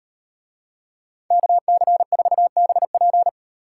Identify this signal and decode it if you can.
Morse code, heard as a keyed tone, about 38 wpm, 700 Hz. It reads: KC4BP